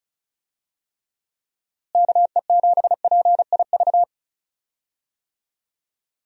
Morse code, heard as a keyed tone, about 36 wpm, 700 Hz. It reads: KE7PIV
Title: USA Call Signs - 6 Characters - Repeated 3x - 35wpm